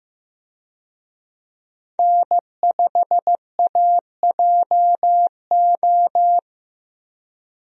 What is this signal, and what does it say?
Morse code, heard as a keyed tone, about 16 wpm, 700 Hz. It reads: N5AJO
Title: USA Call Signs - 5 Characters - Repeated 3x - 15wpm